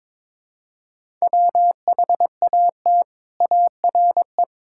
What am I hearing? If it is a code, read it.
Morse code, heard as a keyed tone, about 21 wpm, 700 Hz. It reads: WHAT ARE
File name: practice / 22wpm / Sentences with 3-Word Phrases - 22wpm